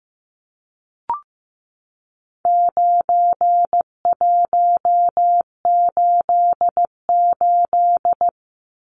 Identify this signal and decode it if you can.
Morse code, heard as a keyed tone, about 15 wpm, 700 Hz. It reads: 9188
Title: Numbers - 4-Digit 15wpm